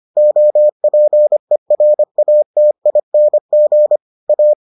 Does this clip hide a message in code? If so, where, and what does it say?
Yes: Morse code from 0.2 to 4.5 s, heard as a keyed tone, about 25 wpm, 600 Hz. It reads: OPERATING A